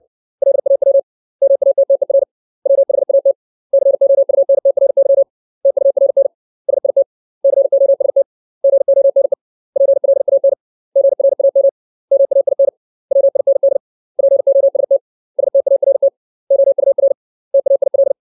Morse code, heard as a keyed tone, about 62 wpm, 550 Hz. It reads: LAW MATTER WHAT COUNTRY TURN SIT COST GONE PLAN GRAM MAIN WIND POST START OUR TAIL
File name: 632 words / 632 Words 60WPM